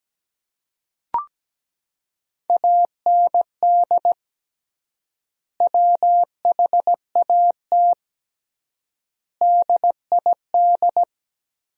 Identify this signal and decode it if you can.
Morse code, heard as a keyed tone, about 17 wpm, 700 Hz. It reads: AND WHAT DID
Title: 3x Word-Spacing - Sentences from Top 300 Words - 17wpm